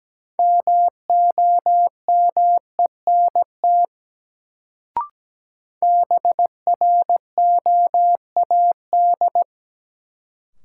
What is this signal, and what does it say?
Morse code, heard as a keyed tone, about 17 wpm, 700 Hz. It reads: MOMENT BROAD
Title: Sentences from Top 1000 Words - New Word Drill - 17wpm